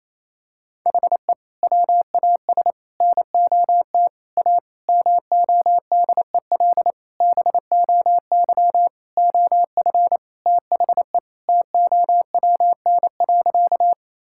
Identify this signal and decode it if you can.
Morse code, heard as a keyed tone, about 28 wpm, 700 Hz. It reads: HE WAS NOT A MODEL BOY OF THE TOWN.
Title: Sentences from Top 1000 Words - Review Part 5 - 28wpm